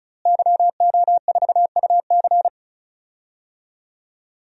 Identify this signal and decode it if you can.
Morse code, heard as a keyed tone, about 36 wpm, 700 Hz. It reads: YO4UC
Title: DXCC-100 Call Signs - 5 Characters - Repeated 2x - 35wpm